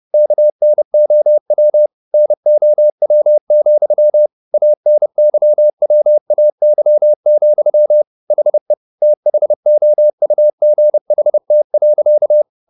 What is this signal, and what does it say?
Morse code, heard as a keyed tone, about 30 wpm, 600 Hz. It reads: KNOW NOW, ANYWAY, HE THOUGHT.
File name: the old man and the sea / the old man and the sea 1 30wpm